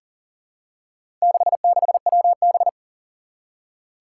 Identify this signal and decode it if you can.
Morse code, heard as a keyed tone, about 42 wpm, 700 Hz. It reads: 66WB
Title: Four Challenging Letter-Number Combinations 40wpm